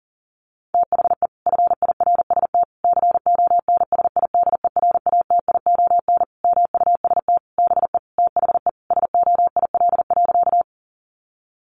Morse code, heard as a keyed tone, about 40 wpm, 700 Hz. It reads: THE FIRST CONSIDERATION MUST BE THE SOIL.
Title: Sentences from Top 1500 Words 40wpm